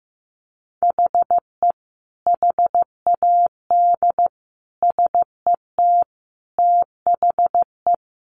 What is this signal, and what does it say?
Morse code, heard as a keyed tone, about 15 wpm, 700 Hz. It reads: HE HAD SET THE
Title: Sentences from Top 1500 Words 15wpm